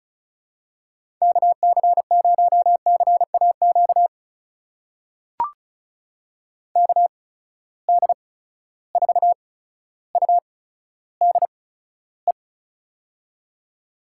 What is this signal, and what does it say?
Morse code, heard as a keyed tone, about 34 wpm, spaced as for 15 wpm, 700 Hz. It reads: KC0CAQ KD4UDE
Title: USA Call Signs - 6 Characters - 8x Character Spacing - 35wpm